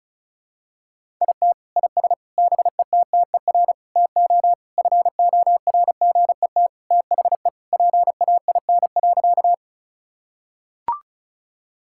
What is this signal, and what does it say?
Morse code, heard as a keyed tone, about 36 wpm, 700 Hz. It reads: IT IS BETTER TO FORGET THE PAIN.
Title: Sentences from Top 1600 Words 35wpm